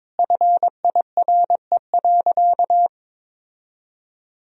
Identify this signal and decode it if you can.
Morse code, heard as a keyed tone, about 22 wpm, 700 Hz. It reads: FIRE.
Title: Sentences from Top 1000 Words - Review Part 3 - 22wpm